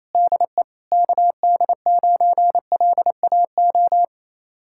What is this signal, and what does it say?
Morse code, heard as a keyed tone, about 34 wpm, 700 Hz. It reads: DE KD9LAO
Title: Speed-Racing Common QSO Phrases Encore - 28wpm